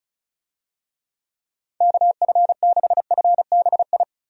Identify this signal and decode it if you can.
Morse code, heard as a keyed tone, about 34 wpm, 700 Hz. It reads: KF6FBI
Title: USA Call Signs - 6 Characters - 35wpm